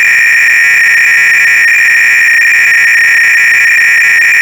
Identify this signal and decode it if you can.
300-baud modem data.
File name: ACR-Audio-Cassette-Programs / CharlesSwiger-Donations / GOBLINS cload-g